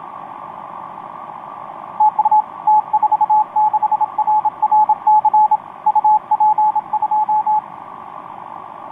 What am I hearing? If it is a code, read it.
Morse code, heard as a keyed tone, about 28 wpm, 860 Hz, with radio noise. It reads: TU T46RRC UP2
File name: t46rrc i2uua etc 12m cw 18feb2013 1528